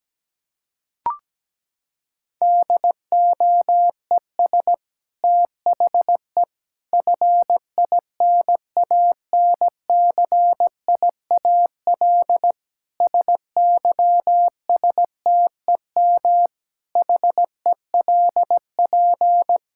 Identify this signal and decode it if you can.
Morse code, heard as a keyed tone, about 17 wpm, 700 Hz. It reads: DOES THE FINANCIAL SYSTEM HELP
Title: Sentences from Top 1000 Words - Review Part 5 - 17wpm